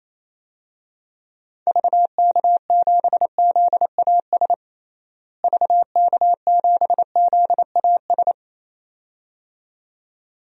Morse code, heard as a keyed tone, about 28 wpm, 700 Hz. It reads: VK7ZAS VK7ZAS
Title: DXCC-100 Call Signs - 6 Characters - Repeated 2x - 28wpm